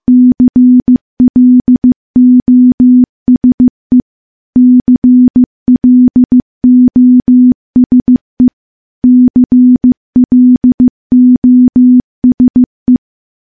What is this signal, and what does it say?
Morse code, heard as a keyed tone, about 15 wpm, 250 Hz. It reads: CLOSE CLOSE CLOSE
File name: morse-training-15WPM